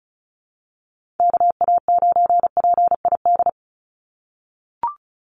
Morse code, heard as a keyed tone, about 34 wpm, 700 Hz. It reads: KA9PID
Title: USA Call Signs 35wpm